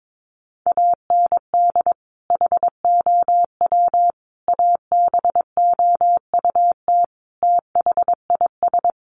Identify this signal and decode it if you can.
Morse code, heard as a keyed tone, about 22 wpm, 700 Hz. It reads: AND HOW ABOUT THIS
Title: Sentences from Top 1200 Words 22wpm